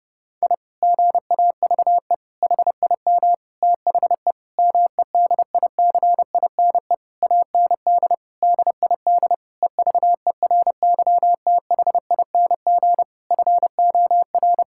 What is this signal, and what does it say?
Morse code, heard as a keyed tone, about 32 wpm, 700 Hz. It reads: I GAVE HIM THE MEDICINE AND DID EVERYTHING FOR
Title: Sentences from Top 1000 Words - Review Part 6 - 30wpm